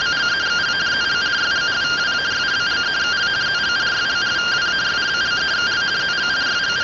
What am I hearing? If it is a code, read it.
Two-tone teleprinter signal, about 170 Hz shift.